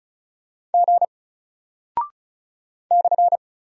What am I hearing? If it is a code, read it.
Morse code, heard as a keyed tone, about 34 wpm, 700 Hz. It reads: G /
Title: Single Character (Letter:Number:Punctuation:Prosign) 35wpm